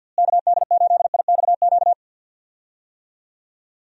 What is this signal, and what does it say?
Morse code, heard as a keyed tone, about 51 wpm, 700 Hz. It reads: KD8IXQ
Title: USA Call Signs - 6 Characters - Repeated 2x - 50wpm